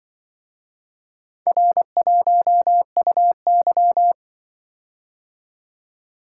Morse code, heard as a keyed tone, about 23 wpm, 700 Hz. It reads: R1UY
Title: Speed-Racing - Four Letter-Number Combinations - 20wpm